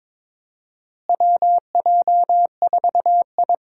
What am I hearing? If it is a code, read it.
Morse code, heard as a keyed tone, about 23 wpm, 700 Hz. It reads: WJ4I
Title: USA Call Signs - 4 Characters - Repeated 3x - 22wpm